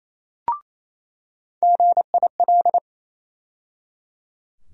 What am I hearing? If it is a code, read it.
Morse code, heard as a keyed tone, about 28 wpm, 700 Hz. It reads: GIL
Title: USA Call Sign Postfixes - 3 Letters 28wpm